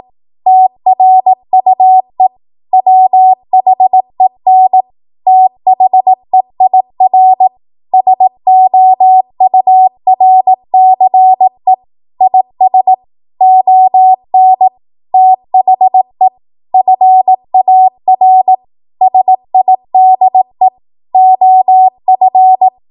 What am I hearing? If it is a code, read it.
Morse code, heard as a keyed tone, about 18 wpm, 750 Hz. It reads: TRUE WHEN THEIR SOURCE IS ON THE FAR SIDE OF
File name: Morse / 18WPM